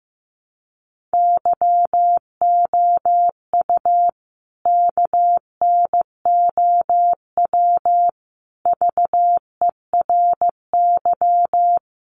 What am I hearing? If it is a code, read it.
Morse code, heard as a keyed tone, about 15 wpm, 700 Hz. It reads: YOU KNOW VERY
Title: A Sound Of Thunder 15wpm